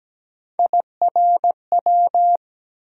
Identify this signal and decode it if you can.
Morse code, heard as a keyed tone, about 17 wpm, 700 Hz. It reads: IRW
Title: USA Call Sign Postfixes - 3 Letters 17wpm